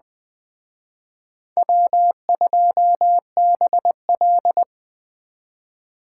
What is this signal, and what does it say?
Morse code, heard as a keyed tone, about 21 wpm, 700 Hz. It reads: W2BL
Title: Speed Racing - USA 1x2 Call Signs 17wpm